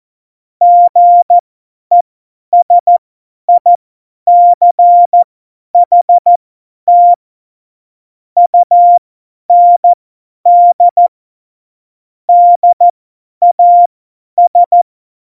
Morse code, heard as a keyed tone, about 14 wpm, spaced as for 10 wpm, 700 Hz. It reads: GESICHT UND DAS
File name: Morsekurs / Witze / Witz 01. Dezember 2020, 50 F70